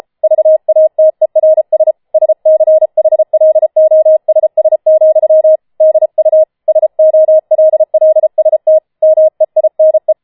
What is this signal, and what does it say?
Morse code, heard as a keyed tone, about 33 wpm, 630 Hz. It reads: VATERS SCHLOSS, DU SOLLST MEINE